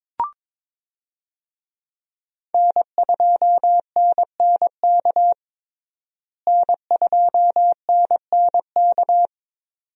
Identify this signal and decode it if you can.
Morse code, heard as a keyed tone, about 22 wpm, 700 Hz. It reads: N2NNK N2NNK
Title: USA Call Signs - 5 Characters - Repeated 2x - 22wpm